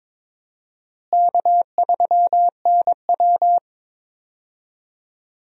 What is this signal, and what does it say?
Morse code, heard as a keyed tone, about 22 wpm, 700 Hz. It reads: K3NW
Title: USA Call Signs - 4 Characters - Repeated 2x - 22wpm